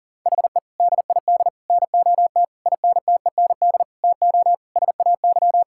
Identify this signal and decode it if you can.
Morse code, heard as a keyed tone, about 40 wpm, 700 Hz. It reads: HE DID NOT INTEND TO SAY
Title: Sentences from Top 1600 Words 40wpm